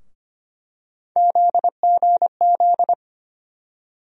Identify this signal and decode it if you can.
Morse code, heard as a keyed tone, about 25 wpm, 700 Hz. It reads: ZGZ
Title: USA Call Sign Postfixes - 3 Letters 25wpm